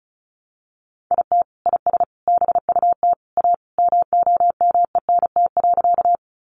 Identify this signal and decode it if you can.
Morse code, heard as a keyed tone, about 34 wpm, 700 Hz. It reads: IT IS BUT A MOMENT.
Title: Sentences from Top 1000 Words 35wpm